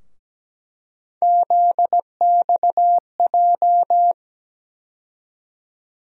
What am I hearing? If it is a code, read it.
Morse code, heard as a keyed tone, about 17 wpm, 700 Hz. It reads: ZXJ
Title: USA Call Sign Postfixes - 3 Letters 17wpm